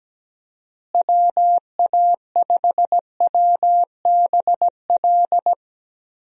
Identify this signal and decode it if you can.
Morse code, heard as a keyed tone, about 17 wpm, 700 Hz. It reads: WA5WBL
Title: USA Call Signs - 6 Characters - Repeated 3x - 17wpm